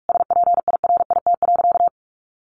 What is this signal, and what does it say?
Morse code, heard as a keyed tone, about 44 wpm, 700 Hz. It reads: SPIRIT.
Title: Sentences from Top 1400 Words 45wpm